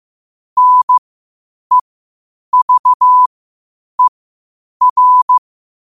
Morse code, heard as a keyed tone, about 15 wpm, spaced as for 8 wpm, 1000 Hz. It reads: NEVER
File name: qotdfiles / 8wpm